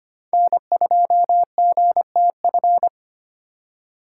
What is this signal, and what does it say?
Morse code, heard as a keyed tone, about 25 wpm, 700 Hz. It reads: N2GTF
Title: USA Call Signs - 5 Characters - 3x Character Spacing - 25wpm